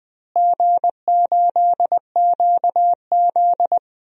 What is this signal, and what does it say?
Morse code, heard as a keyed tone, about 21 wpm, 700 Hz. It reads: G8QZ
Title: DXCC-100 Call Signs - 4 Characters - Repeated 2x - 20wpm